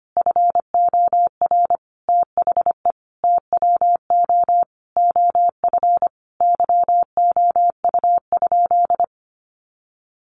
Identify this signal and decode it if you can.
Morse code, heard as a keyed tone, about 25 wpm, 700 Hz. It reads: FOR THE TWO OF YOU?
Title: Sentences from Top 1400 Words 25wpm